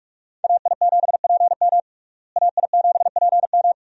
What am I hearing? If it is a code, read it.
Morse code, heard as a keyed tone, about 46 wpm, 700 Hz. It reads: AI7PM AI7PM
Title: USA Call Signs - 5 Characters - Repeated 3x - 45wpm